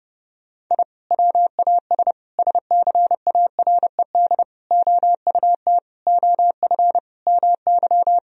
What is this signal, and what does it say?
Morse code, heard as a keyed tone, about 30 wpm, 700 Hz. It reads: I WAS SCARED OUT OF MY
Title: Sentences from Top 1000 Words - Review Part 4 - 30wpm